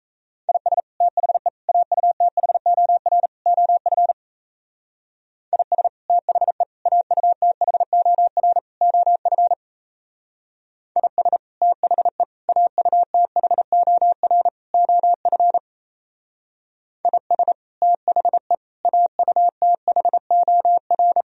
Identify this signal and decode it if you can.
Morse code, heard as a keyed tone, about 36 wpm, 700 Hz. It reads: IS THE AUTHOR OF IS THE AUTHOR OF IS THE AUTHOR OF IS THE AUTHOR